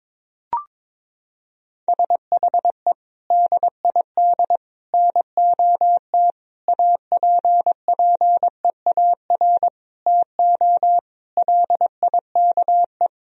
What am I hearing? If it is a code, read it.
Morse code, heard as a keyed tone, about 22 wpm, 700 Hz. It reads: SHE DID NOT APPEAR TO LIKE
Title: Sentences from Top 1000 Words - Review Part 2 - 22wpm